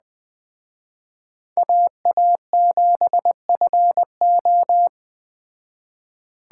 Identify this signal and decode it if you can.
Morse code, heard as a keyed tone, about 21 wpm, 700 Hz. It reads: AA7FO
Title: Speed Racing - USA 2x2 Call Signs 17wpm